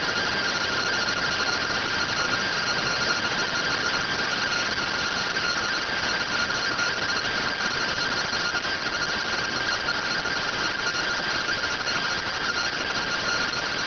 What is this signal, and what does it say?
Two-tone teleprinter signal, about 170 Hz shift.